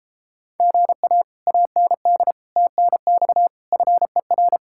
Morse code, heard as a keyed tone, about 33 wpm, 700 Hz. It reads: GA AND TNX FER